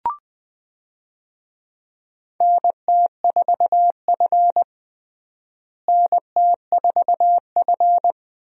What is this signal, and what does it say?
Morse code, heard as a keyed tone, about 21 wpm, 700 Hz. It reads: NT4F NT4F
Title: USA Call Signs - 4 Characters - Repeated 2x - 20wpm